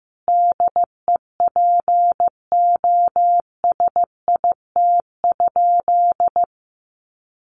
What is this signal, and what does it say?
Morse code, heard as a keyed tone, about 15 wpm, 700 Hz. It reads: DEPOSIT?
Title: Sentences from Top 1300 Words 15wpm